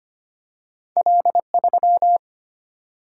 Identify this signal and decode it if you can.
Morse code, heard as a keyed tone, about 25 wpm, 700 Hz. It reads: L3
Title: Single-Letter Single-Number 25wpm